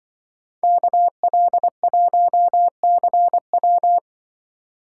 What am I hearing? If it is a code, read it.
Morse code, heard as a keyed tone, about 23 wpm, 700 Hz. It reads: KL1CW
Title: Speed Racing - USA Call Signs 20wpm